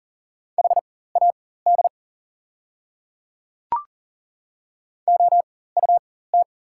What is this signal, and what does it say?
Morse code, heard as a keyed tone, about 40 wpm, spaced as for 19 wpm, 690 Hz. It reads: HAD OUT
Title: 4x Character Spacing - Top 100 Words - 40wpm